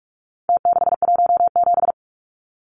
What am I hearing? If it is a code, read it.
Morse code, heard as a keyed tone, about 44 wpm, 700 Hz. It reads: T617
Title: Numbers - Contest Style - 4-Digit 45wpm